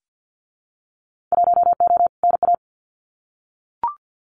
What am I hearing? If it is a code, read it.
Morse code, heard as a keyed tone, about 50 wpm, 700 Hz. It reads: 1O NA
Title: Winter Field Day Exchanges 50wpm